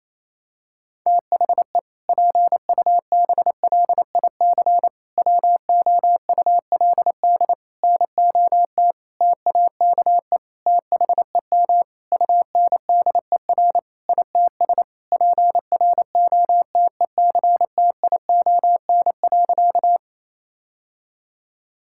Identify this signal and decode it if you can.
Morse code, heard as a keyed tone, about 28 wpm, 700 Hz. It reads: THE PUBLIC WOULD NOT TAKE THEM UNDER ITS PROTECTION.
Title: Sentences from Top 1600 Words 28wpm